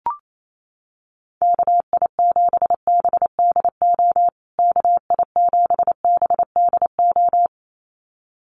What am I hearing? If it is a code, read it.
Morse code, heard as a keyed tone, about 28 wpm, 700 Hz. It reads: KI7BDO KI7BDO
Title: USA Call Signs Repeated 2x 28wpm